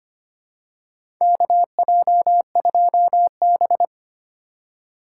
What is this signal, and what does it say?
Morse code, heard as a keyed tone, about 25 wpm, 700 Hz. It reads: KJ2B
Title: USA Call Signs - 3x Character Spacing - 25wpm